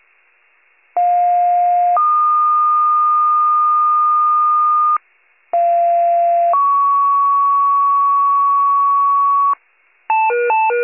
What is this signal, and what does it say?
Two-tone page, heard as a one-second tone, then a three-second tone, about 710 then 1220 Hz, about 690 then 1120 Hz. It is heard from 1.0 to 9.5 s.
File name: wcalert